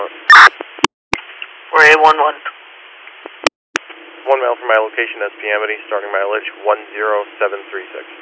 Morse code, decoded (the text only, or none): none